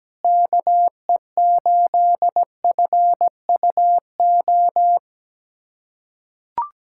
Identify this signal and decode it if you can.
Morse code, heard as a keyed tone, about 17 wpm, 700 Hz. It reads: KE8FUO
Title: USA Callsign DE Callsign - 6 Characters - 17wpm